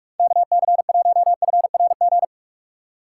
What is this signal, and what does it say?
Morse code, heard as a keyed tone, about 46 wpm, 700 Hz. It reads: KC1FRG
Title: USA Call Signs - 6 Characters - 3x Character Spacing - 45wpm